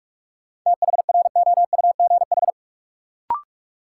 Morse code, heard as a keyed tone, about 43 wpm, 700 Hz. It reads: THROUGH
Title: Sentences from Top 200 Words - New Word Drill - 45wpm